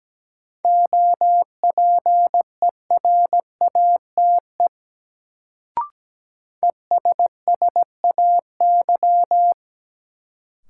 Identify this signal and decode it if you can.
Morse code, heard as a keyed tone, about 17 wpm, 700 Hz. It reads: OPERATE ESSAY
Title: Sentences from Top 1500 Words - New Word Drill - 17wpm